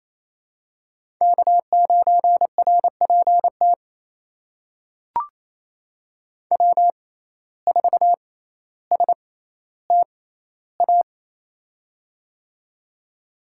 Morse code, heard as a keyed tone, about 28 wpm, spaced as for 14 wpm, 700 Hz. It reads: K9RPT W4STA
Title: USA Call Signs - 5 Characters - 6x Character Spacing - 28wpm